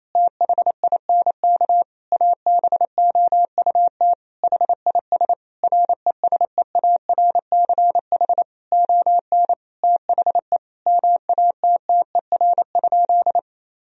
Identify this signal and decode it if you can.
Morse code, heard as a keyed tone, about 28 wpm, 700 Hz. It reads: THINK ABOUT HIS RESEARCH ON THE MATTER?
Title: Sentences from Top 1000 Words - Review Part 4 - 28wpm